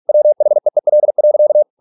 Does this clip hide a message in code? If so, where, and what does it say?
Yes: Morse code from 0.1 to 1.6 s, heard as a keyed tone, about 46 wpm, 600 Hz. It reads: WHEEL.